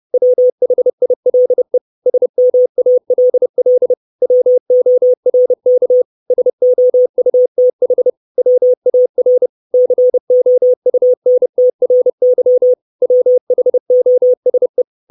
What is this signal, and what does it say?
Morse code, heard as a keyed tone, about 30 wpm, 500 Hz. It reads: WHILE SMALL WORK SOUTH WAR COUNTRY WHOSE
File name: 632 Words 30WPM